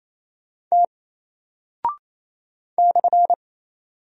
Morse code, heard as a keyed tone, about 28 wpm, 700 Hz. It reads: T /
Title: Single Character (Letter:Number:Punctuation:Prosign) Alt 28wpm